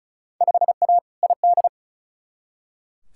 Morse code, heard as a keyed tone, about 34 wpm, 700 Hz. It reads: 5A ID
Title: Field Day Exchanges Encore - 35wpm